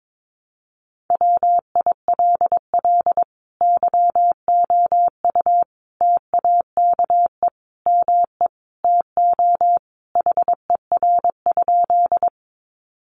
Morse code, heard as a keyed tone, about 22 wpm, 700 Hz. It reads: WILL YOU TAKE ME TO HER?
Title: Sentences from Top 200 Words 22wpm